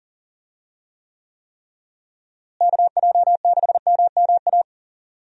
Morse code, heard as a keyed tone, about 40 wpm, 700 Hz. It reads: KJ6MMA
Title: USA Call Signs - 6 Characters - Repeated 3x - 40wpm